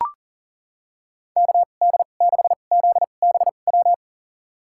Morse code, heard as a keyed tone, about 40 wpm, spaced as for 29 wpm, 700 Hz. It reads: KD6ZBW
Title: USA Call Signs - 6 Characters - 2x Character Spacing - 40wpm